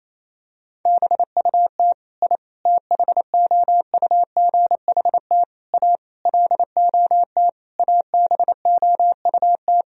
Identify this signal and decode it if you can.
Morse code, heard as a keyed tone, about 28 wpm, 700 Hz. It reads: BUT I THOUGHT A LOT ABOUT
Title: Sentences from Top 1000 Words - Review Part 3 - 28wpm